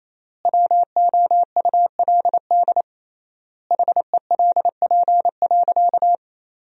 Morse code, heard as a keyed tone, about 28 wpm, 700 Hz. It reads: WOULD HELP.
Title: 3x Word-Spacing - Sentences from Top 200 Words - 28wpm